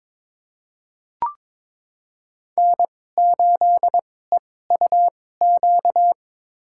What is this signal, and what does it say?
Morse code, heard as a keyed tone, about 22 wpm, spaced as for 16 wpm, 700 Hz. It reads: N8EUQ
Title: USA Call Signs - 5 Characters - 2x Character Spacing - 22wpm